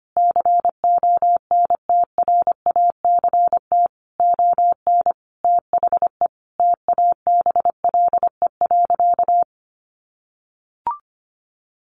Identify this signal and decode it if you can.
Morse code, heard as a keyed tone, about 25 wpm, 700 Hz. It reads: CONTRACT ON THE TABLE.
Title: Sentences from Top 1400 Words 25wpm